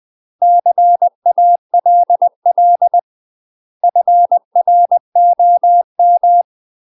Morse code, heard as a keyed tone, about 20 wpm, 700 Hz. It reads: CALL FROM
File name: UK MASTER QSOs TRAINING / 20WPM / UK MASTER QSOs 0005